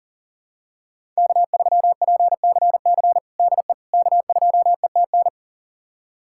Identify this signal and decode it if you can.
Morse code, heard as a keyed tone, about 40 wpm, 700 Hz. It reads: K3PCC DE K2ETN
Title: USA Callsign DE Callsign - 5 Characters - 40wpm